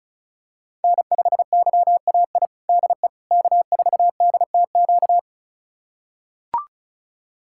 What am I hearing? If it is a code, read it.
Morse code, heard as a keyed tone, about 36 wpm, 700 Hz. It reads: N5YAI DE K4DTQ
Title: USA Callsign DE Callsign 35wpm